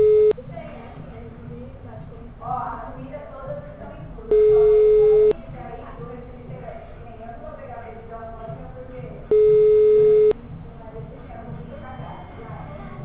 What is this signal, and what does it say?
Ringback tone (Europe, 425 Hz), with speech.
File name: out- -2004- - - .28207